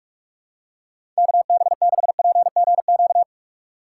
Morse code, heard as a keyed tone, about 44 wpm, 700 Hz. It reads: KB6PGQ
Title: USA Call Signs - 6x Character Spacing - 45wpm